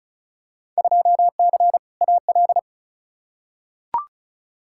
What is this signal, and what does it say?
Morse code, heard as a keyed tone, about 36 wpm, 700 Hz. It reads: 2C AL
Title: Field Day Exchanges Encore - 35wpm